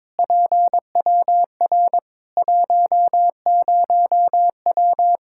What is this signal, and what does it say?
Morse code, heard as a keyed tone, about 22 wpm, 700 Hz. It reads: PWR 10W
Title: Practice QSO - Exchange 2 - Info - 22wpm